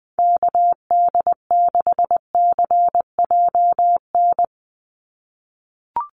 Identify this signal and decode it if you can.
Morse code, heard as a keyed tone, about 20 wpm, 700 Hz. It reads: KD6CJN
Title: USA Call Signs 20wpm